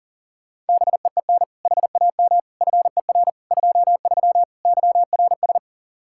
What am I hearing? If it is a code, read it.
Morse code, heard as a keyed tone, about 49 wpm, 700 Hz. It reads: BEEN HAM FER 23 YRS
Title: Speed-Racing Common QSO Phrases - 40wpm